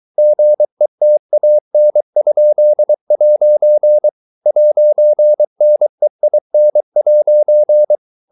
Morse code, heard as a keyed tone, about 23 wpm, 600 Hz. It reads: GETAN?' 'NEIN'